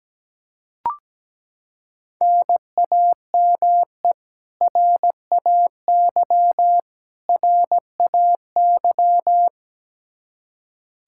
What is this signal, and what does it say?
Morse code, heard as a keyed tone, about 17 wpm, 700 Hz. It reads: NAME RAY RAY
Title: Practice QSO - Exchange 1 - Beginning - 17wpm